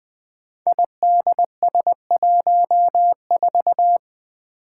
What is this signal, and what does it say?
Morse code, heard as a keyed tone, about 20 wpm, 700 Hz. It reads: IDS14
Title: Five Challenging Letter-Number Combinations 20wpm